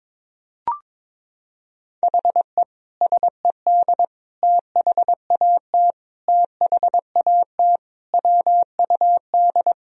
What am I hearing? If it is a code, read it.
Morse code, heard as a keyed tone, about 22 wpm, 700 Hz. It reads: HE SED THAT THAT WUD
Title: Sentences from Top 200 Words and Abbreviations 22wpm